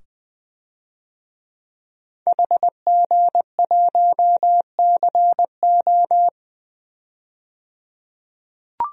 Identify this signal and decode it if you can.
Morse code, heard as a keyed tone, about 21 wpm, 700 Hz. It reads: HG1CO
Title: DXCC-100 Call Signs - 5 Characters - 20wpm